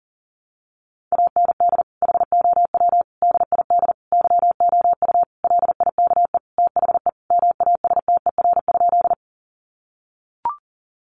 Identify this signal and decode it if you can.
Morse code, heard as a keyed tone, about 40 wpm, 700 Hz. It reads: AND HOW DID YOU LIKE THE MASTER?
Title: Sentences from Top 1300 Words 40wpm